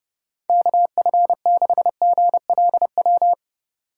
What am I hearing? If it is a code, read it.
Morse code, heard as a keyed tone, about 31 wpm, 700 Hz. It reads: KF6GLW
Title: USA Call Signs - 6 Characters - 2x Character Spacing - 30wpm